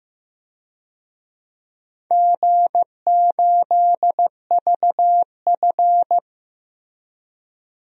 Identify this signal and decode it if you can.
Morse code, heard as a keyed tone, about 16 wpm, 700 Hz. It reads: G8VF
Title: DXCC-100 Call Signs - 4 Characters - Repeated 2x - 15wpm